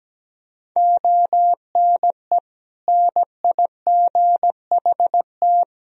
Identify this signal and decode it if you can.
Morse code, heard as a keyed tone, about 17 wpm, 700 Hz. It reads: ONE NIGHT
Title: Sentences from Top 300 Words and Abbreviations 17wpm